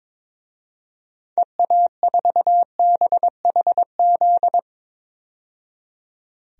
Morse code, heard as a keyed tone, about 22 wpm, 700 Hz. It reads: EA4BHZ
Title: DXCC-100 Call Signs - 6 Characters - 22wpm